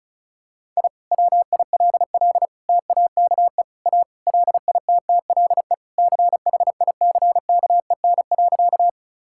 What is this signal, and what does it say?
Morse code, heard as a keyed tone, about 36 wpm, 690 Hz. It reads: I WILL TAKE A LITTLE CHICKEN.
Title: Sentences from Top 1000 Words - Review Part 5 - 35wpm